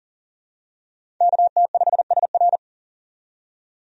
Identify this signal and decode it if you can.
Morse code, heard as a keyed tone, about 40 wpm, 700 Hz. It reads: KT5SR
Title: USA Call Signs - 5 Characters - 40wpm